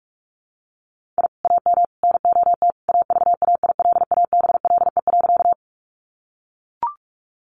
Morse code, heard as a keyed tone, about 44 wpm, 700 Hz. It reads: I AM NOT AVAILABLE.
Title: Sentences from Top 600 Words 45wpm